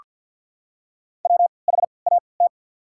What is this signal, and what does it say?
Morse code, heard as a keyed tone, about 48 wpm, spaced as for 28 wpm, 700 Hz. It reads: WHAT
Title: 3x Character Spacing - Top 100 Words - 50wpm